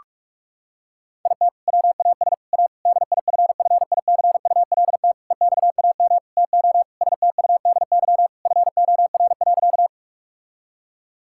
Morse code, heard as a keyed tone, about 47 wpm, 700 Hz. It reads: IT WAS A DIFFICULT EXAM TO STUDY FOR.
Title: Sentences from Top 1000 Words - Review Part 5 - 45wpm